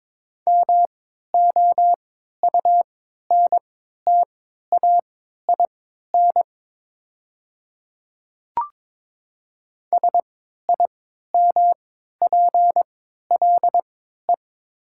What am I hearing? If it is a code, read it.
Morse code, heard as a keyed tone, about 22 wpm, spaced as for 13 wpm, 700 Hz. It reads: MOUNTAIN SIMPLE
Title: 3x Character Spacing - Top 500 Words - 22wpm